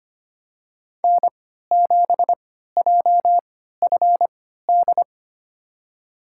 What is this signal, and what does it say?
Morse code, heard as a keyed tone, about 25 wpm, spaced as for 14 wpm, 700 Hz. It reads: N7JFD
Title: USA Call Signs - 5 Characters - 3x Character Spacing - 25wpm